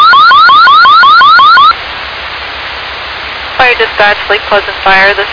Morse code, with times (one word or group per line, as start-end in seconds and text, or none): none